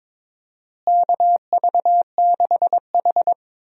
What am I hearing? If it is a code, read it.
Morse code, heard as a keyed tone, about 22 wpm, 700 Hz. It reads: KV6H
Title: USA Call Signs - 4 Characters - Repeated 3x - 22wpm